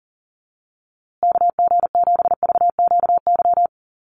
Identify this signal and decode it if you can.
Morse code, heard as a keyed tone, about 40 wpm, 700 Hz. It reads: KG7VQY
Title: USA Call Signs Repeated 3x 40wpm